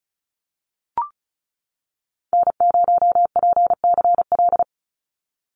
Morse code, heard as a keyed tone, about 34 wpm, 700 Hz. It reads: N0PCL
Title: NA SOTA Activator Callsigns 35wpm